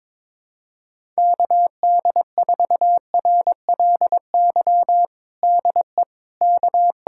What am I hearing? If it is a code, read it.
Morse code, heard as a keyed tone, about 22 wpm, 700 Hz. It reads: KD4RLY DE K
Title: USA Callsign DE Callsign 22wpm